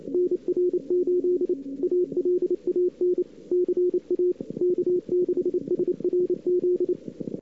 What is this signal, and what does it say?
Morse code, heard as a keyed tone, about 28 wpm, 360 Hz. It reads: NR8 ALAN CA K6SRZ
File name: CWSprint / Feb2007 / 2007-feb-cw-sprint-k5tr-op-n5rz -00:00